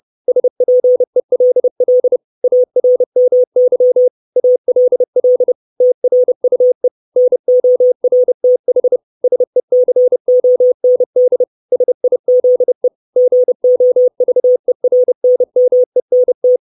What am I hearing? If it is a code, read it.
Morse code, heard as a keyed tone, about 30 wpm, 500 Hz. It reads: SPELL ARMY ALL TRUE NORTH SECOND SIZE GOVERNMENT
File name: 632 Words 30WPM